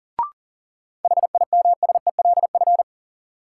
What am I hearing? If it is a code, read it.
Morse code, heard as a keyed tone, about 40 wpm, 700 Hz. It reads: HIMSELF
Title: Sentences from Top 1500 Words - New Word Drill - 40wpm